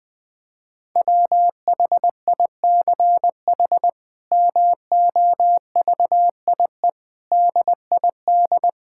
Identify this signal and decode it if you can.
Morse code, heard as a keyed tone, about 20 wpm, 700 Hz. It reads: WHICH MOVIE DID
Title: Sentences from Top 1000 Words - Review Part 5 - 20wpm